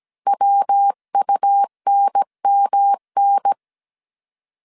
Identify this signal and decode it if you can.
Morse code, heard as a keyed tone, about 17 wpm, 800 Hz. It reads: WUNMN